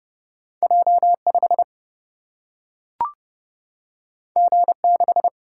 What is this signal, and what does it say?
Morse code, heard as a keyed tone, about 31 wpm, 700 Hz. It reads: J 5 G 6
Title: Single-Letter Single-Number 30wpm